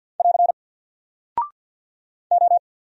Morse code, heard as a keyed tone, about 49 wpm, 700 Hz. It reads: <AR> O
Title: Single Character (Letter:Number:Punctuation:Prosign) Alt 50wpm